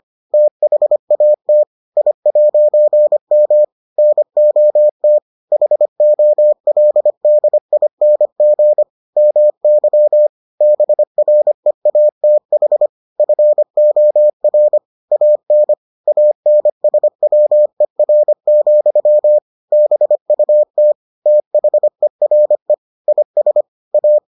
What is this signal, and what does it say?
Morse code, heard as a keyed tone, about 25 wpm, 600 Hz. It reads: THAT I'M NOT HOLDING MY BREATH FOR AN ANSWER, BUT THERE IS A